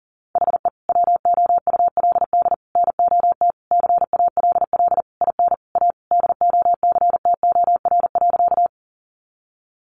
Morse code, heard as a keyed tone, about 40 wpm, 700 Hz. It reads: HE WOULD NOT CALL IN A DOCTOR.
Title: Sentences from Top 1500 Words 40wpm